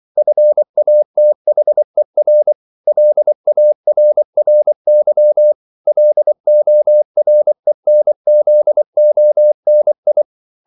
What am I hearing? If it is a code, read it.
Morse code, heard as a keyed tone, about 24 wpm, 600 Hz. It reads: FATHER LARRY LORENZONI
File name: CW / Practice / EnglishQuotations0424